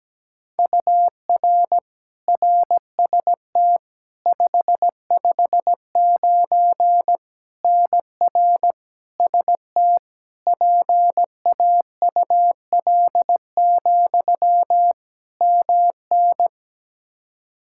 Morse code, heard as a keyed tone, about 17 wpm, 700 Hz. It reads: UR RST 559 NR ST PAUL, MN
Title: Practice QSO - Exchange 1 - Beginning - 17wpm